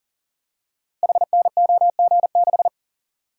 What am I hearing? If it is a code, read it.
Morse code, heard as a keyed tone, about 42 wpm, 690 Hz. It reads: HNOG6
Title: Code Groups - 5 Letter:Numbers - 40wpm